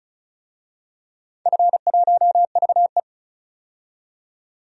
Morse code, heard as a keyed tone, about 36 wpm, 700 Hz. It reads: F1VE
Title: DXCC-100 Call Signs - 4 Characters - 35wpm